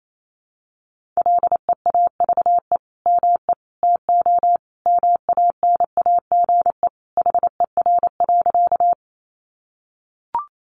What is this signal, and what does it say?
Morse code, heard as a keyed tone, about 28 wpm, 700 Hz. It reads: LEAVE ME TO MANAGE HER.
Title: Sentences from Top 1400 Words 28wpm